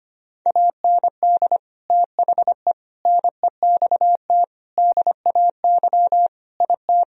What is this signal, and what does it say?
Morse code, heard as a keyed tone, about 25 wpm, 700 Hz. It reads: AND THE NEXT DAY IT
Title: Sentences from Top 1000 Words - Review Part 1 - 25wpm